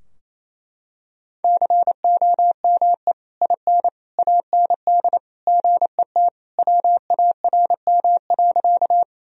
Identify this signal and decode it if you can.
Morse code, heard as a keyed tone, about 27 wpm, 700 Hz. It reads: COME IN AND GET WARM.
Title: Sentences from Top 1000 Words - Review Part 2 - 28wpm